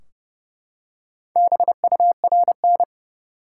Morse code, heard as a keyed tone, about 31 wpm, 700 Hz. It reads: BURN
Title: Four Letter Words - 30wpm